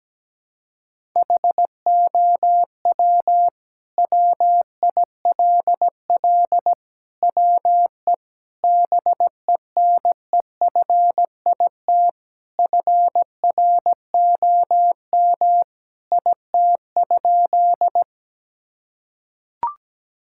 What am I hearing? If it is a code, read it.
Morse code, heard as a keyed tone, about 17 wpm, 700 Hz. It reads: HOW WILL WE BENEFIT FROM IT?
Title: Sentences from Top 1000 Words - Review Part 6 - 17wpm